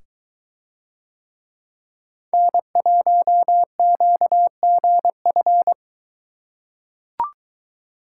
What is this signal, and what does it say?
Morse code, heard as a keyed tone, about 22 wpm, 700 Hz. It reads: N1QGF